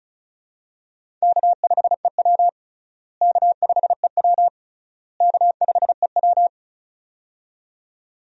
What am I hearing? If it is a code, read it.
Morse code, heard as a keyed tone, about 34 wpm, 700 Hz. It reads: K5EW K5EW K5EW
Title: USA Call Signs - 4 Characters - Repeated 3x - 35wpm